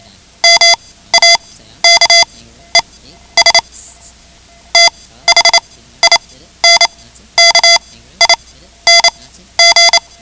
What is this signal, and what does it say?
Morse code, heard as a keyed tone, about 28 wpm, spaced as for 14 wpm, 2050 Hz. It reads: MAKES THINKING
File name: ham radio / CodeQuick / Code Quick 10